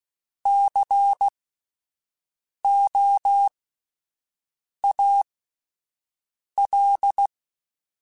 Morse code, heard as a keyed tone, about 16 wpm, spaced as for 5 wpm, 800 Hz. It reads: COAL
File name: Spaced Alphas / Spaced Alphas mp3s / Cs sp @10